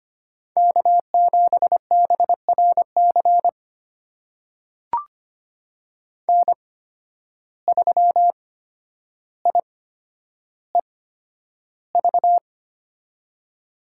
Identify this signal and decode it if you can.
Morse code, heard as a keyed tone, about 25 wpm, spaced as for 11 wpm, 700 Hz. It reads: K7BRC N3IEV
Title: USA Call Signs - 8x Character Spacing - 25wpm